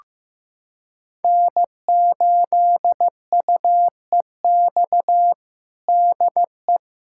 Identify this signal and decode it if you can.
Morse code, heard as a keyed tone, about 15 wpm, 700 Hz. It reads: N8UEX DE
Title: USA Callsign DE Callsign - 5 Characters - 15wpm